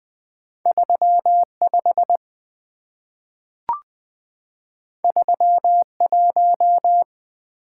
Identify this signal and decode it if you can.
Morse code, heard as a keyed tone, about 20 wpm, 700 Hz. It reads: 35 31
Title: Two Letter-Number Combinations - 20wpm